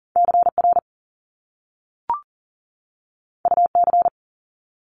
Morse code, heard as a keyed tone, about 41 wpm, 700 Hz. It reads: CR UC
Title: Top Two Letter Combinations in Words 40wpm